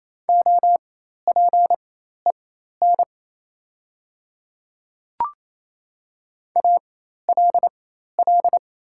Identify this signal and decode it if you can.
Morse code, heard as a keyed tone, about 28 wpm, spaced as for 13 wpm, 690 Hz. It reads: OPEN ALL
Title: 4x Character Spacing - Top 300 Words - 28wpm